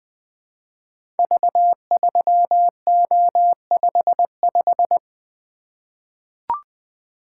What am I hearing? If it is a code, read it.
Morse code, heard as a keyed tone, about 21 wpm, 700 Hz. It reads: V3O55
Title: Code Groups - 5 Letter:Numbers - 20wpm